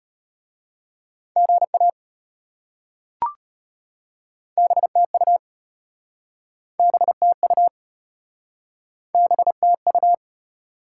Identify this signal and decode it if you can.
Morse code, heard as a keyed tone, about 33 wpm, 700 Hz. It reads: GA BTU BTU BTU